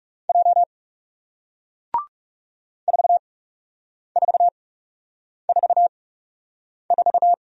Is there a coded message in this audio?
Yes: Morse code from 0.3 to 7.3 s, heard as a keyed tone, about 40 wpm, spaced as for 8 wpm, 700 Hz. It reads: J 4444